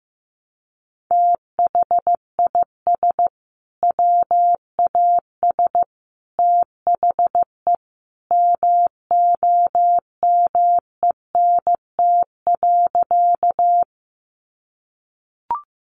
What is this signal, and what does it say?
Morse code, heard as a keyed tone, about 15 wpm, 700 Hz. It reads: THIS WAS THE MOMENT.
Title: Sentences from Top 1000 Words 15wpm